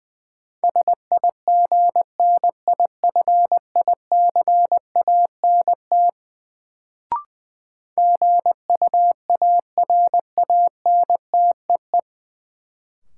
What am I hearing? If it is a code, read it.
Morse code, heard as a keyed tone, about 20 wpm, 700 Hz. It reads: SIGNIFICANT GUARANTEE
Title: Sentences from Top 1600 Words - New Word Drill Alt - 20wpm